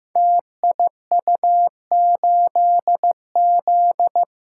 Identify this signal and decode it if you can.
Morse code, heard as a keyed tone, about 15 wpm, 700 Hz. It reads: TIU8Z
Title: Code Groups - 5 Letter:Numbers - 15wpm